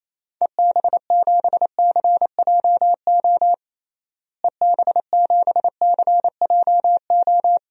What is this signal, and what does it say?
Morse code, heard as a keyed tone, about 27 wpm, 700 Hz. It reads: EB7CJO EB7CJO
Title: DXCC-100 Call Signs - 6 Characters - Repeated 3x - 28wpm